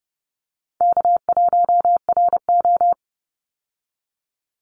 Morse code, heard as a keyed tone, about 30 wpm, 700 Hz. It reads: K1RO
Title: Call Signs Repeated 3x - 30wpm